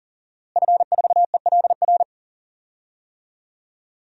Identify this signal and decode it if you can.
Morse code, heard as a keyed tone, about 40 wpm, 700 Hz. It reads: F4ELR
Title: DXCC-100 Call Signs - 5 Characters - Repeated 3x - 40wpm